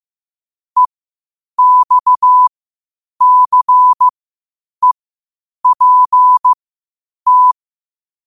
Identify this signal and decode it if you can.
Morse code, heard as a keyed tone, about 15 wpm, spaced as for 8 wpm, 1000 Hz. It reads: EXCEPT
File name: qotdfiles / 8wpm